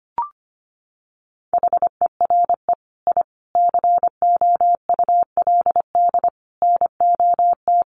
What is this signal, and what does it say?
Morse code, heard as a keyed tone, about 25 wpm, 700 Hz. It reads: HERE I COULD NOT
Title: Sentences from Top 1400 Words 25wpm